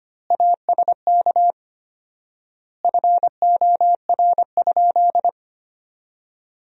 Morse code, heard as a keyed tone, about 25 wpm, 700 Hz. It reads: ASK FOR?
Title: 4x Word-Spacing - Sentences from Top 200 Words - 25wpm